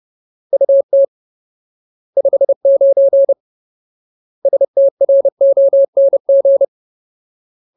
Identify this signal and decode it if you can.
Morse code, heard as a keyed tone, about 30 wpm, 550 Hz. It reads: UT 59 STRONG
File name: Ultimate Word List / 30WPM Ultimate